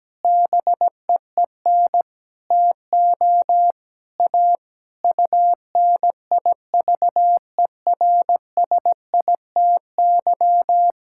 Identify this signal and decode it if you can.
Morse code, heard as a keyed tone, about 17 wpm, 700 Hz. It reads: BEEN TO A UNIVERSITY
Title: Sentences from Top 1000 Words - Review Part 5 - 17wpm